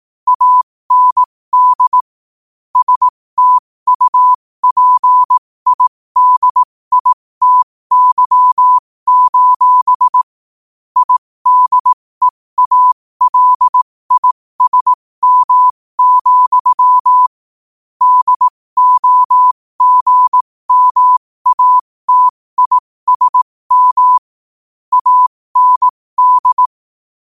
Morse code, heard as a keyed tone, about 18 wpm, spaced as for 15 wpm, 1000 Hz. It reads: AND STUPIDITY: IDEALISM, DOGMATISM AND
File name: qotdfiles / 15wpm